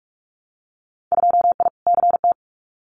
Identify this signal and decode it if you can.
Morse code, heard as a keyed tone, about 46 wpm, 700 Hz. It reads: 2I CT
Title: Winter Field Day Exchanges 45wpm